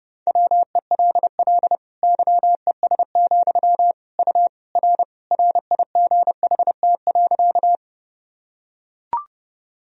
Morse code, heard as a keyed tone, about 30 wpm, 700 Hz. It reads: WELL YES, U R RIGHT.
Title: Sentences from Top 500 Words and Abbreviations 30wpm